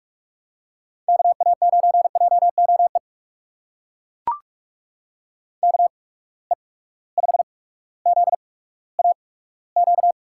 Morse code, heard as a keyed tone, about 42 wpm, spaced as for 19 wpm, 700 Hz. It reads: KA9JOE KE5ZAQ
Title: USA Call Signs - 6 Characters - 8x Character Spacing - 45wpm